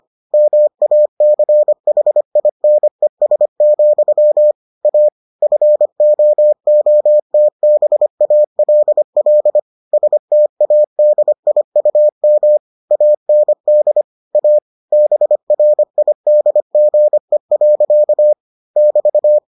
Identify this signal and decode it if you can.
Morse code, heard as a keyed tone, about 25 wpm, 600 Hz. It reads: MACHINES, A FOOTBALL STADIUM AND A BRIDGE. <BT>